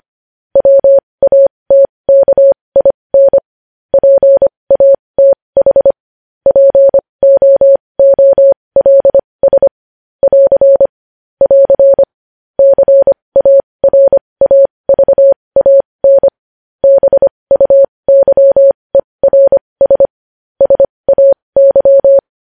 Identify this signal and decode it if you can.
Morse code, heard as a keyed tone, about 25 wpm, spaced as for 20 wpm, 550 Hz. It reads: WATKIN PATH POOLS <AR> <AR> CARAVAN BUYERS SAY
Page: News headlines in Morse code at 20 wpm